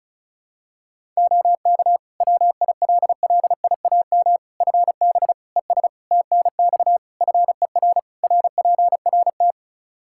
Morse code, heard as a keyed tone, about 34 wpm, 700 Hz. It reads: OK WILLIAM FB ES TNX FER RPRT
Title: Practice QSO - Exchange 2 - Info - 35wpm